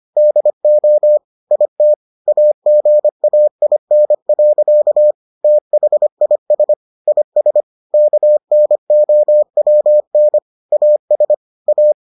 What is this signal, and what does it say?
Morse code, heard as a keyed tone, about 25 wpm, 600 Hz. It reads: DO IT AGAIN. THIS IS KNOWN AS A